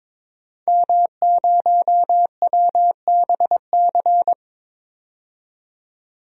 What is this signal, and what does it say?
Morse code, heard as a keyed tone, about 22 wpm, 700 Hz. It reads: M0WBC
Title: DXCC-100 Call Signs - 5 Characters - 22wpm